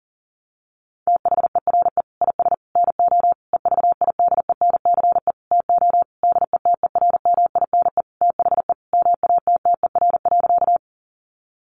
Morse code, heard as a keyed tone, about 40 wpm, 700 Hz. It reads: THERE IS NO EVIDENCE TO DETERMINE THE MATTER.
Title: Sentences from Top 1300 Words 40wpm